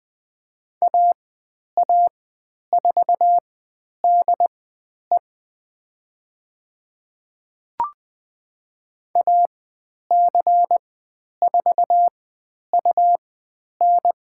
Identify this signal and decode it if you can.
Morse code, heard as a keyed tone, about 20 wpm, spaced as for 10 wpm, 700 Hz. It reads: AA4DE AC4UN
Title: 102 Lesson 17g - Letter u - Call Signs 20wpm